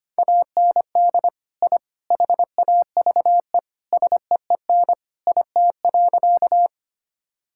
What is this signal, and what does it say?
Morse code, heard as a keyed tone, about 25 wpm, 700 Hz. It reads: AND I HAVE SEEN IT.
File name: practice / 25wpm / Sentences with 3-Word Phrases - 25wpm